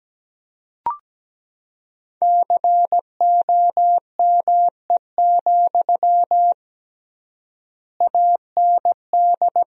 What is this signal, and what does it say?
Morse code, heard as a keyed tone, about 17 wpm, 700 Hz. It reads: COME, AND
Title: 3x Word-Spacing - Sentences from Top 500 Words - 17wpm